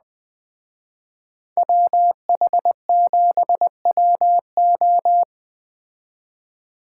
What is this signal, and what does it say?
Morse code, heard as a keyed tone, about 20 wpm, 700 Hz. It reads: WH7WO